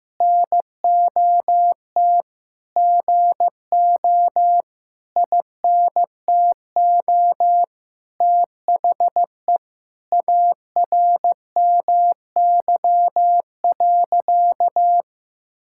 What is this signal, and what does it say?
Morse code, heard as a keyed tone, about 15 wpm, 700 Hz. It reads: NOT GO INTO THE ARMY.
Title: Sentences from Top 1000 Words - Review Part 5 - 15wpm